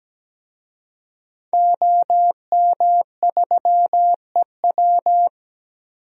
Morse code, heard as a keyed tone, about 17 wpm, 700 Hz. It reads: OM3EW
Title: DXCC-100 Call Signs - 5 Characters - 17wpm